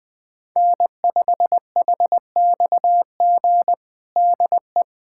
Morse code, heard as a keyed tone, about 20 wpm, 700 Hz. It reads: N5HXG DE
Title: USA Callsign DE Callsign - 5 Characters - 20wpm